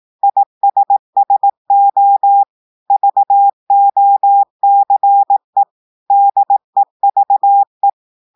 Morse code, heard as a keyed tone, about 18 wpm, 800 Hz. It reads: ISSO VOCE DEVE
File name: audios / portugais / portugais texte en clair 2 vitesse 18 mots minutes